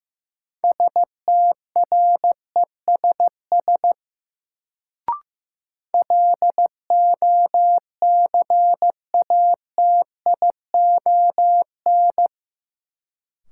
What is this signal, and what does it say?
Morse code, heard as a keyed tone, about 15 wpm, 700 Hz. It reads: STRESS LOCATION
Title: Sentences from Top 1000 Words - Review Part 6 - New Word Drill - 15wpm